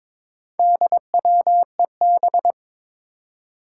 Morse code, heard as a keyed tone, about 22 wpm, 700 Hz. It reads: DWEB
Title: Four Challenging Letter-Number Combinations 22wpm